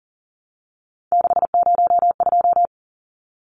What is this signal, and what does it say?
Morse code, heard as a keyed tone, about 40 wpm, 700 Hz. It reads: 602
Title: Numbers - 3-Digit 40wpm